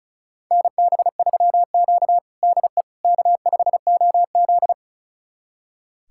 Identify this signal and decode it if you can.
Morse code, heard as a keyed tone, about 34 wpm, 700 Hz. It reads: NB3Q DE K5OZ
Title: USA Callsign DE Callsign - 4 Characters - 35wpm